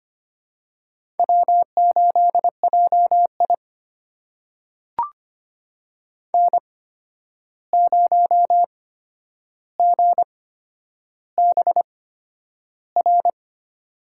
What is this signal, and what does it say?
Morse code, heard as a keyed tone, about 25 wpm, spaced as for 10 wpm, 700 Hz. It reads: W8JI N0GBR
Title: USA Call Signs - 8x Character Spacing - 25wpm